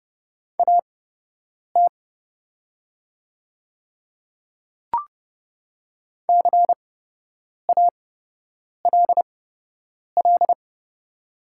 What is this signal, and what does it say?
Morse code, heard as a keyed tone, about 30 wpm, spaced as for 8 wpm, 700 Hz. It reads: AT CALL
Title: 8x Character Spacing - Top 100 Words - 30wpm